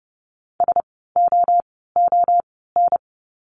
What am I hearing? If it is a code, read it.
Morse code, heard as a keyed tone, about 30 wpm, 700 Hz. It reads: S O O N
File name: Lesson 16f - Letter c - Words 30wpm